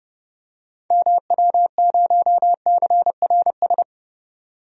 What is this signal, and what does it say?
Morse code, heard as a keyed tone, about 32 wpm, 700 Hz. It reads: MW0CRS
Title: DXCC-100 Call Signs - 6 Characters - Repeated 2x - 30wpm